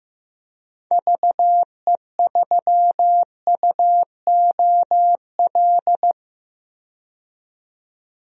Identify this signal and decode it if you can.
Morse code, heard as a keyed tone, about 15 wpm, 700 Hz. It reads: VE3UOL
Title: DXCC-100 Call Signs - 6 Characters - 15wpm